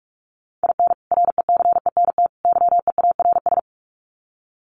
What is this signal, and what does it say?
Morse code, heard as a keyed tone, about 44 wpm, 700 Hz. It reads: IN RECENT YEARS
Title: Sets of 3 Words 45wpm